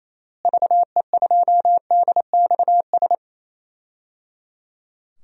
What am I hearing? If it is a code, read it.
Morse code, heard as a keyed tone, about 28 wpm, 700 Hz. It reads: VE2DXS
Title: DXCC-100 Call Signs - 6 Characters - Repeated 3x - 28wpm